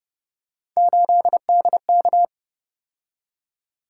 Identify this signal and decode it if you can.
Morse code, heard as a keyed tone, about 28 wpm, 700 Hz. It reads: 8DK